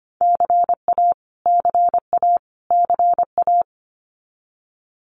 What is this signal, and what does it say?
Morse code, heard as a keyed tone, about 25 wpm, 700 Hz. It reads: CA CA CA
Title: US States Repeated 3x 25wpm